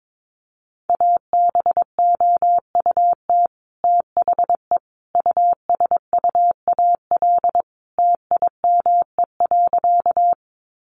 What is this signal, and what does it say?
Morse code, heard as a keyed tone, about 22 wpm, 700 Hz. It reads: ABOUT THE USUAL TIME.
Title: Sentences from Top 400 Words 22wpm